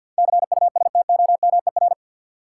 Morse code, heard as a keyed tone, about 47 wpm, 700 Hz. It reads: CUSTOMER
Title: Sentences from Top 1100 Words - New Word Drill - 50wpm